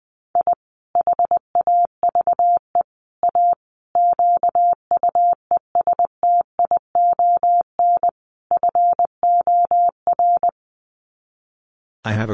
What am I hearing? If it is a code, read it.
Morse code, heard as a keyed tone, about 20 wpm, 700 Hz. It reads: I HAVE A QUESTION FOR
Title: Sets of 5 Words Encore2 20wpm